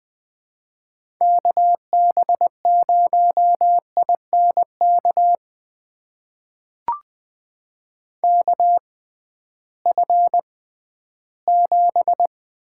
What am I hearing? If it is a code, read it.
Morse code, heard as a keyed tone, about 20 wpm, spaced as for 13 wpm, 700 Hz. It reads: KB0INK KF7
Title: USA Call Signs - 6 Characters - 6x Character Spacing - 20wpm